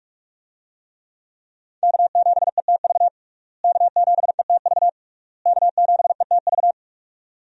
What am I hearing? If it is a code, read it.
Morse code, heard as a keyed tone, about 41 wpm, 700 Hz. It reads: K7ETV K7ETV K7ETV
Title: USA Call Signs - 5 Characters - Repeated 3x - 45wpm